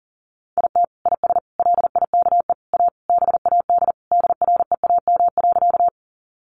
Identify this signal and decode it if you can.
Morse code, heard as a keyed tone, about 40 wpm, 700 Hz. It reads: IT IS LIKE A BAD DREAM.
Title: Sentences from Top 1500 Words 40wpm